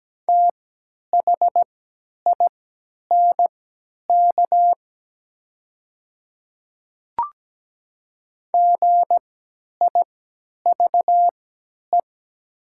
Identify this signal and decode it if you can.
Morse code, heard as a keyed tone, about 17 wpm, spaced as for 10 wpm, 690 Hz. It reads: THINK GIVE
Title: 3x Character Spacing - Top 500 Words - 17wpm